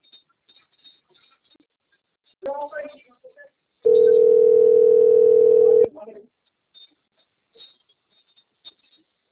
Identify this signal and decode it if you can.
Ringback tone (North America), 440 + 480 Hz, with speech.
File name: out- - - - - .4655